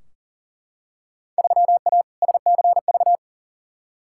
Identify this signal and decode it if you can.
Morse code, heard as a keyed tone, about 40 wpm, 700 Hz. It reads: 3A SCV
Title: Field Day Exchanges - 40wpm